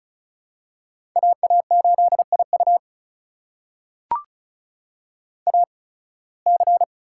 Morse code, heard as a keyed tone, about 34 wpm, spaced as for 23 wpm, 700 Hz. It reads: AA8IU AC
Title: USA Call Signs - 5 Characters - 8x Character Spacing - 35wpm